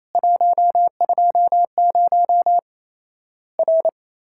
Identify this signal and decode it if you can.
Morse code, heard as a keyed tone, about 28 wpm, 700 Hz. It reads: 120 R
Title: Worked All Europe DX Contest - QTC Exchange - 28wpm